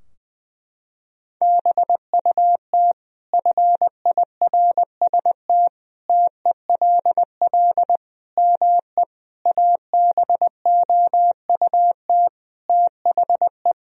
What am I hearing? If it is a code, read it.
Morse code, heard as a keyed tone, about 21 wpm, 700 Hz. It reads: BUT FIRST TELL ME ABOUT THE
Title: Sentences from Top 1000 Words - Review Part 2 - 20wpm